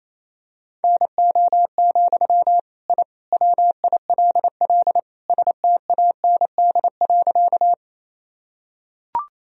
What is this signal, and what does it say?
Morse code, heard as a keyed tone, about 28 wpm, 700 Hz. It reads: NO, I WILL STAND.
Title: Sentences from Top 1000 Words - Review Part 1 - 28wpm